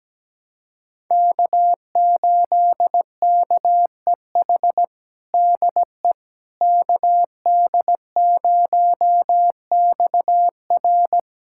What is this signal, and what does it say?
Morse code, heard as a keyed tone, about 17 wpm, 700 Hz. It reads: K8KEH DE KD0XR
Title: USA Callsign DE Callsign - 5 Characters - 17wpm